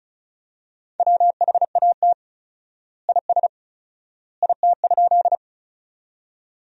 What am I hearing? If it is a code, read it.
Morse code, heard as a keyed tone, about 36 wpm, 700 Hz. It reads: WHAT IS IT?
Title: 4x Word-Spacing - Sentences from Top 100 Words - 35wpm